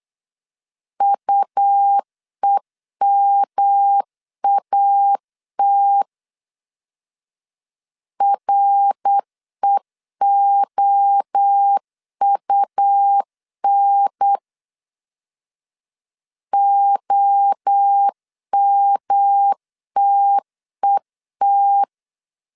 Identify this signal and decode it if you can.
Morse code, heard as a keyed tone, about 8 wpm, 800 Hz. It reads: UEMAT REOUN OMTET
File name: CW / DET / 1 3 1500C S B 8wpm 4141